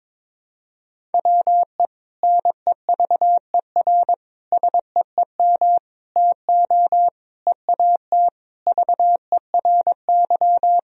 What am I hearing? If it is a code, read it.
Morse code, heard as a keyed tone, about 22 wpm, 700 Hz. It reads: WE NEVER SEEM TO EAT VERY
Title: Sentences from Top 1000 Words - Review Part 1 - 22wpm